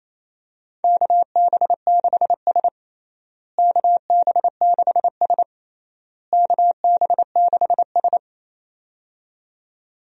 Morse code, heard as a keyed tone, about 27 wpm, 700 Hz. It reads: KB6S KB6S KB6S
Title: USA Call Signs - 4 Characters - Repeated 3x - 28wpm